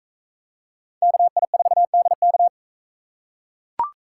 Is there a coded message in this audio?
Yes: Morse code from 1.0 to 2.5 s, heard as a keyed tone, about 43 wpm, 700 Hz. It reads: KI4DK